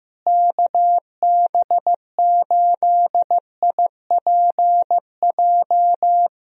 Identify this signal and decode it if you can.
Morse code, heard as a keyed tone, about 15 wpm, 700 Hz. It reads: KB8IPJ
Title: USA Call Signs - 6 Characters - Repeated 3x - 15wpm